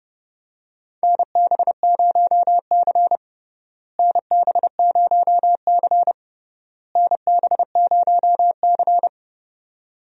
Morse code, heard as a keyed tone, about 29 wpm, 700 Hz. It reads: NB0C NB0C NB0C
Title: USA Call Signs - 4 Characters - Repeated 3x - 30wpm